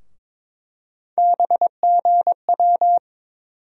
Morse code, heard as a keyed tone, about 22 wpm, 700 Hz. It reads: BGW
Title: USA Call Sign Postfixes - 3 Letters 22wpm